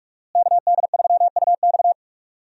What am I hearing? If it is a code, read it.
Morse code, heard as a keyed tone, about 46 wpm, 700 Hz. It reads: KD3UX
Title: USA Call Signs - 5 Characters - Repeated 3x - 45wpm